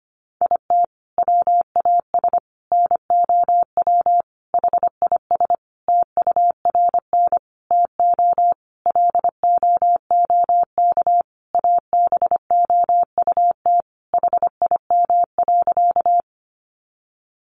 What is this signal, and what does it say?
Morse code, heard as a keyed tone, about 25 wpm, 700 Hz. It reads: IT WAS NOW HIS TURN TO LOOK ABOUT HIM.
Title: Sentences from Top 200 Words 25wpm